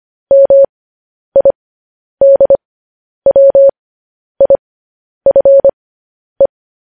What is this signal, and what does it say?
Morse code, heard as a keyed tone, about 25 wpm, spaced as for 10 wpm, 550 Hz. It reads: MIDWIFE
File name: podcast / News headlines-1x-25x10